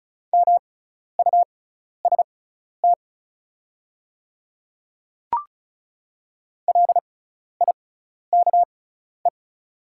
Morse code, heard as a keyed tone, about 36 wpm, spaced as for 12 wpm, 700 Hz. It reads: MUST LIKE
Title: 6x Character Spacing - Top 500 Words - 35wpm